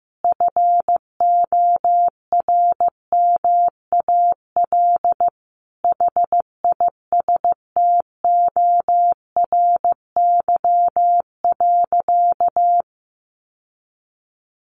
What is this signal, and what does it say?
Morse code, heard as a keyed tone, about 15 wpm, 700 Hz. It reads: FORMAL HISTORY.
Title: Sentences from Top 1300 Words 15wpm